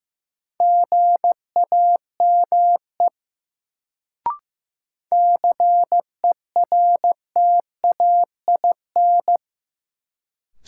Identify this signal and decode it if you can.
Morse code, heard as a keyed tone, about 15 wpm, 700 Hz. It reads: GAME CERTAIN
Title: Sentences from Top 500 Words - New Word Drill - 15wpm